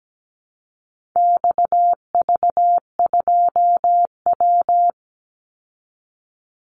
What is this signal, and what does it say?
Morse code, heard as a keyed tone, about 17 wpm, 700 Hz. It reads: XV2W
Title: Call Signs Repeated 3x - 17wpm